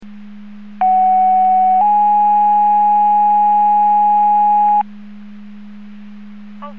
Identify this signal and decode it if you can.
Two-tone page, heard as a one-second tone, then a three-second tone, about 750 then 830 Hz. It is heard from 0.8 to 4.8 s.
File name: Portage 154.385 Paging